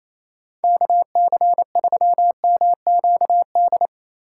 Morse code, heard as a keyed tone, about 37 wpm, 700 Hz. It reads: KC3MQD